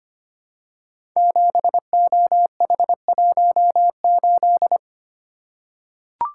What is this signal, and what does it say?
Morse code, heard as a keyed tone, about 25 wpm, 700 Hz. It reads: 7OH18
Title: Code Groups - 5 Letter:Numbers - 25wpm